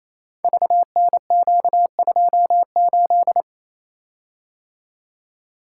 Morse code, heard as a keyed tone, about 29 wpm, 700 Hz. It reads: VNQ28
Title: Code Groups - 5 Letter:Numbers - 28wpm